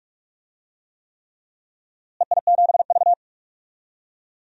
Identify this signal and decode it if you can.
Morse code, heard as a keyed tone, about 47 wpm, 700 Hz. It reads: EI7V
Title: DXCC-100 Call Signs - 4 Characters - Repeated 3x - 45wpm